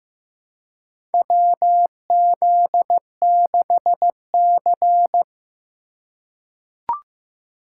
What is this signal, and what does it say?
Morse code, heard as a keyed tone, about 15 wpm, 700 Hz. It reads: WZ6C
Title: USA Call Signs - 4 Characters - 4x Character Spacing - 15wpm